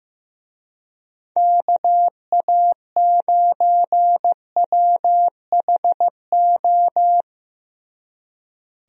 Morse code, heard as a keyed tone, about 15 wpm, 700 Hz. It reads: KA9WHO
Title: USA Call Signs - 6 Characters - Repeated 2x - 15wpm